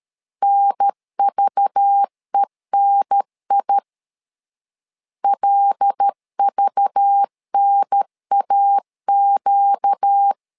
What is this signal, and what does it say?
Morse code, heard as a keyed tone, about 13 wpm, 800 Hz. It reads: NVENI LVNAQ
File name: LEZIONI / 5 LBVC / 5 LBV 12wpm 1855